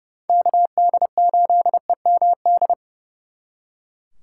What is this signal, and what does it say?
Morse code, heard as a keyed tone, about 34 wpm, 700 Hz. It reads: KD8EMD
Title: Speed Racing - USA Call Signs 30wpm